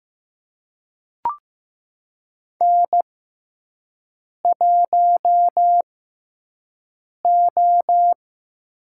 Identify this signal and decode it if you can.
Morse code, heard as a keyed tone, about 15 wpm, spaced as for 5 wpm, 700 Hz. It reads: N1O
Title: USA Call Signs - 6x Character Spacing - 15wpm